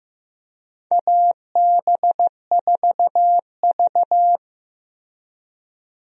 Morse code, heard as a keyed tone, about 15 wpm, 700 Hz. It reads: AB4V
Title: USA Call Signs - 4 Characters - Repeated 2x - 15wpm